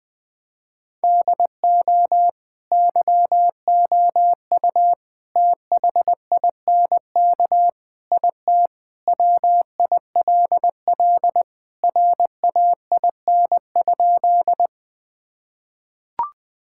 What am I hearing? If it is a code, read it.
Morse code, heard as a keyed tone, about 21 wpm, 700 Hz. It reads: DO YOU THINK IT WILL RAIN?
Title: Sentences with 4-Word Phrases - 20wpm